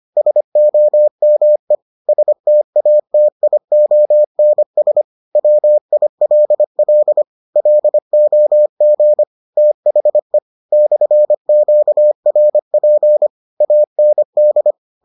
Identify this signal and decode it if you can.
Morse code, heard as a keyed tone, about 25 wpm, 600 Hz. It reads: SOME STATIONS WILL LOG THE /QRP AND